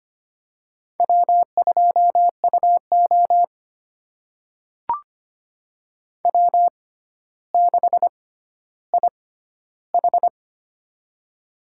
Morse code, heard as a keyed tone, about 25 wpm, spaced as for 13 wpm, 700 Hz. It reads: W2UO W6IH
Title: USA Call Signs - 4 Characters - 6x Character Spacing - 25wpm